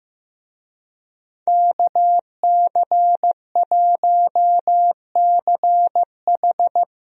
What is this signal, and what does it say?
Morse code, heard as a keyed tone, about 15 wpm, 700 Hz. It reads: KC1CH
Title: USA Call Signs - 5 Characters - Repeated 3x - 15wpm